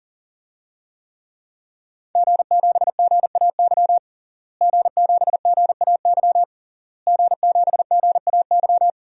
Morse code, heard as a keyed tone, about 40 wpm, 700 Hz. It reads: G7GAY G7GAY G7GAY
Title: DXCC-100 Call Signs - 5 Characters - Repeated 3x - 40wpm